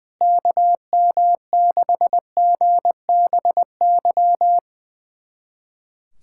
Morse code, heard as a keyed tone, about 21 wpm, 700 Hz. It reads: KM6GBY
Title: USA Callsign DE Callsign - 6 Characters - 20wpm